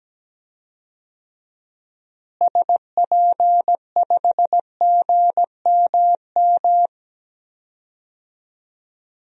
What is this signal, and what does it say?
Morse code, heard as a keyed tone, about 17 wpm, 700 Hz. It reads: SP5GMM
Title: DXCC-100 Call Signs - 6 Characters - Repeated 2x - 17wpm